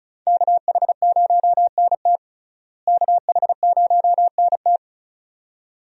Morse code, heard as a keyed tone, about 34 wpm, 700 Hz. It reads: KH0NT KH0NT
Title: USA Call Signs - 5 Characters - Repeated 2x - 35wpm